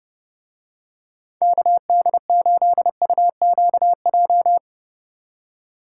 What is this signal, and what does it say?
Morse code, heard as a keyed tone, about 30 wpm, 700 Hz. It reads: KD8UQJ